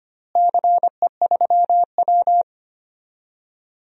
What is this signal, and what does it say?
Morse code, heard as a keyed tone, about 25 wpm, 700 Hz. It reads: CE3W
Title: DXCC-100 Call Signs - 4 Characters - Repeated 3x - 25wpm